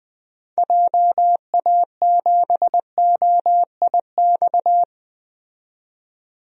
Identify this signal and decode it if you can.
Morse code, heard as a keyed tone, about 20 wpm, 700 Hz. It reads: JA7OIX
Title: DXCC-100 Call Signs - 6 Characters - 20wpm